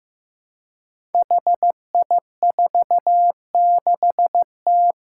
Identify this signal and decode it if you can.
Morse code, heard as a keyed tone, about 15 wpm, 700 Hz. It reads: HI46T
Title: Five Challenging Letter-Number Combinations 15wpm